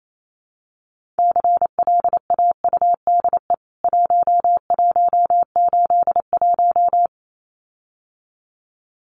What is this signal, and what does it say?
Morse code, heard as a keyed tone, about 28 wpm, 700 Hz. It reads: CLAUDE 1181
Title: CWT Exchange 28wpm